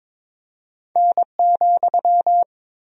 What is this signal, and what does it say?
Morse code, heard as a keyed tone, about 22 wpm, 700 Hz. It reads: N,
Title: Sentences from Top 100 Words and Abbreviations 22wpm